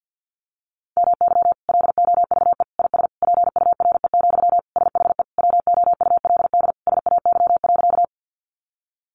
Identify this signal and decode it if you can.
Morse code, heard as a keyed tone, about 50 wpm, 700 Hz. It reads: MY LOVE IS PURE, SHE WOULD SAY.
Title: Sentences from Top 1500 Words 50wpm